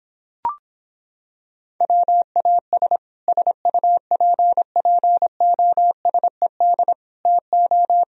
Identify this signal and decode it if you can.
Morse code, heard as a keyed tone, about 26 wpm, 700 Hz. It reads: WAS SUPPOSED TO